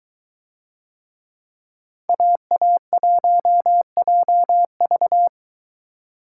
Morse code, heard as a keyed tone, about 22 wpm, 700 Hz. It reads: AA1JV